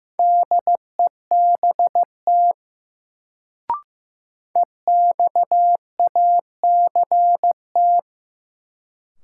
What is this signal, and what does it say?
Morse code, heard as a keyed tone, about 16 wpm, 700 Hz. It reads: DEBT EXACT
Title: Sentences from Top 1300 Words - New Word Drill - 15wpm